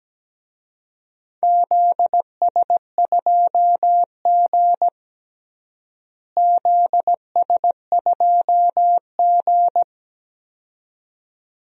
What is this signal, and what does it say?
Morse code, heard as a keyed tone, about 17 wpm, 700 Hz. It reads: ZS2G ZS2G
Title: DXCC-100 Call Signs - 4 Characters - Repeated 2x - 17wpm